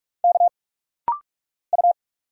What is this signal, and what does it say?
Morse code, heard as a keyed tone, about 46 wpm, 700 Hz. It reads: K U
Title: Single Letters - Rapid-Fire 45wpm